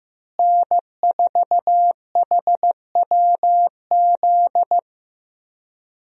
Morse code, heard as a keyed tone, about 15 wpm, 700 Hz. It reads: N4HWZ
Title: USA Call Signs - 5 Characters - Repeated 2x - 15wpm